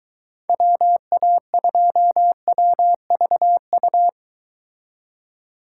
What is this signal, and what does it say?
Morse code, heard as a keyed tone, about 23 wpm, 700 Hz. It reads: WA2WVU